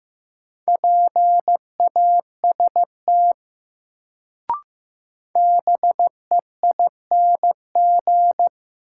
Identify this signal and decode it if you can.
Morse code, heard as a keyed tone, about 15 wpm, 700 Hz. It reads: PAST BEING
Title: Sentences from Top 600 Words - New Word Drill - 15wpm